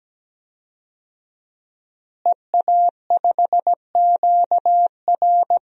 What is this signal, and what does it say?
Morse code, heard as a keyed tone, about 17 wpm, 700 Hz. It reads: EA5QR
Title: DXCC-100 Call Signs - 5 Characters - Repeated 2x - 17wpm